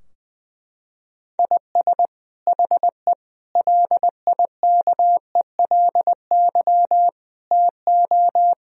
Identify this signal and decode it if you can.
Morse code, heard as a keyed tone, about 20 wpm, 700 Hz. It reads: IS HE LIKELY TO
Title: Sentences from Top 1000 Words - Review Part 4 - 20wpm